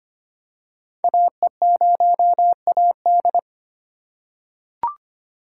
Morse code, heard as a keyed tone, about 25 wpm, 700 Hz. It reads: AE0AD
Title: Code Groups - 5 Letter:Numbers - 25wpm